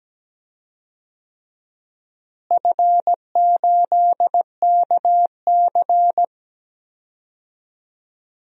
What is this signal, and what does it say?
Morse code, heard as a keyed tone, about 17 wpm, 700 Hz. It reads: F8KC
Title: DXCC-100 Call Signs - 4 Characters - Repeated 3x - 17wpm